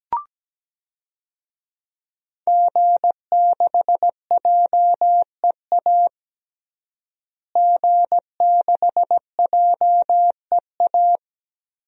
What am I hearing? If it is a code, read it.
Morse code, heard as a keyed tone, about 17 wpm, 700 Hz. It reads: G6JEA G6JEA
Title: DXCC-100 Call Signs - 5 Characters - Repeated 3x - 17wpm